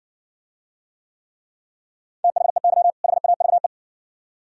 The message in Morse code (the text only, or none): THEY HAVE